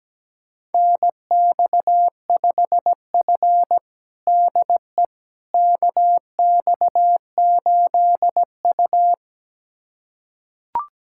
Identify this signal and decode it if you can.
Morse code, heard as a keyed tone, about 17 wpm, 700 Hz. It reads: NX5F DE KX8U
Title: USA Callsign DE Callsign - 4 Characters - 17wpm